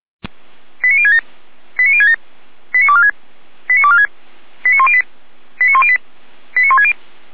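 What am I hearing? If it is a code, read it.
Five-tone selective call (ZVEI), heard as quick runs of five tones, tones of about 70 ms. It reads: 89067 89067 89246 89246 89108 89108 89160
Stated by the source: ZVEI five-tone selective call 89078 received by SDR.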